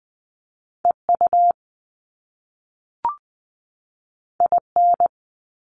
Morse code, heard as a keyed tone, about 20 wpm, 700 Hz. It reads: E U I N
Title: Top Two Letter Combinations in Words 20wpm